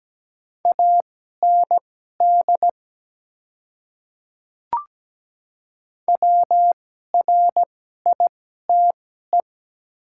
Morse code, heard as a keyed tone, about 17 wpm, spaced as for 12 wpm, 700 Hz. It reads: AND WRITE
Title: 2x Character Spacing - Top 100 Words - 17wpm